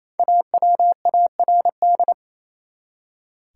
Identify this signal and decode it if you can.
Morse code, heard as a keyed tone, about 28 wpm, 700 Hz. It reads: AWARD
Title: Five Letter Words - 28wpm